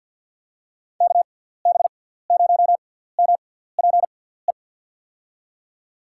Morse code, heard as a keyed tone, about 51 wpm, spaced as for 17 wpm, 700 Hz. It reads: KB0MPE
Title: USA Call Signs - 6 Characters - 6x Character Spacing - 50wpm